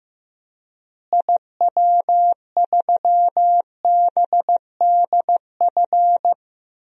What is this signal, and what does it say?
Morse code, heard as a keyed tone, about 15 wpm, 700 Hz. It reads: IW3BDF
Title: DXCC-100 Call Signs - 6 Characters - Repeated 3x - 15wpm